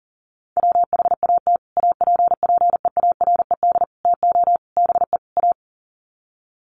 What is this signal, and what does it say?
Morse code, heard as a keyed tone, about 40 wpm, 700 Hz. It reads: WHAT APPEARED TO BE A
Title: Sets of 5 Words Encore2 40wpm